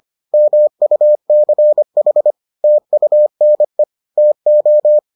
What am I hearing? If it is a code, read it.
Morse code, heard as a keyed tone, about 25 wpm, 600 Hz. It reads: MUCH TUNE TO